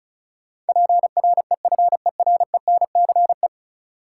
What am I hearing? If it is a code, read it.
Morse code, heard as a keyed tone, about 36 wpm, 700 Hz. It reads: PREFERENCE
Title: Sentences from Top 1500 Words - New Word Drill Alt - 35wpm